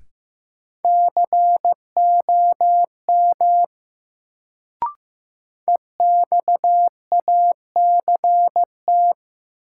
Morse code, heard as a keyed tone, about 15 wpm, 700 Hz. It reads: COM EXACT
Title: Sentences from Top 1300 Words - New Word Drill - 15wpm